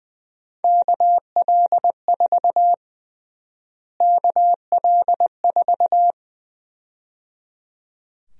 Morse code, heard as a keyed tone, about 20 wpm, 700 Hz. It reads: KL4 KL4
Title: USA Call Sign Prefixes - Repeated 2x - 20wpm